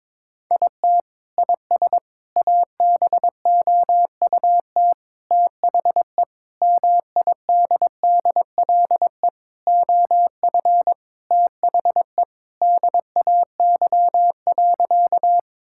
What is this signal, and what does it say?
Morse code, heard as a keyed tone, about 22 wpm, 700 Hz. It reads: IT IS ABOUT THE MIDDLE OF THE DAY.
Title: Sentences from Top 1000 Words - Review Part 6 - 22wpm